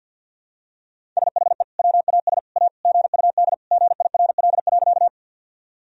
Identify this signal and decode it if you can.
Morse code, heard as a keyed tone, about 51 wpm, 700 Hz. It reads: SHE WAS A GUD GIRL.
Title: Sentences from Top 400 Words and Abbreviations 50wpm